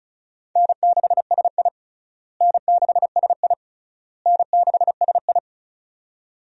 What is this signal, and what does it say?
Morse code, heard as a keyed tone, about 34 wpm, 700 Hz. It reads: N6SI N6SI N6SI
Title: USA Call Signs - 4 Characters - Repeated 3x - 35wpm